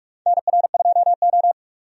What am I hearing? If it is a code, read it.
Morse code, heard as a keyed tone, about 46 wpm, 700 Hz. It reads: NR2O
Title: USA Call Signs - 4 Characters - 45wpm